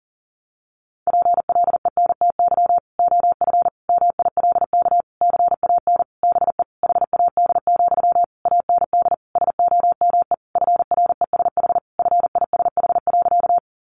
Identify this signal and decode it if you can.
Morse code, heard as a keyed tone, about 40 wpm, 700 Hz. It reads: PLENTY OF MILK CAN BE HAD, AND SOME FRESH FISH.
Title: Sentences from Top 1500 Words 40wpm